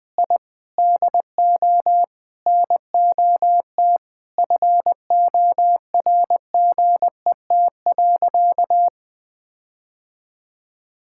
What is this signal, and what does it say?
Morse code, heard as a keyed tone, about 21 wpm, 700 Hz. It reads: I DO NOT FORGET.
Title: Sentences with 4-Word Phrases - Spoken Beginning - 20wpm